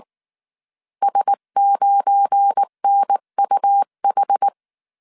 Morse code, heard as a keyed tone, about 19 wpm, 800 Hz. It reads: S9NUH
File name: CW / DET / 1 9 1500C S B 18wpm 2551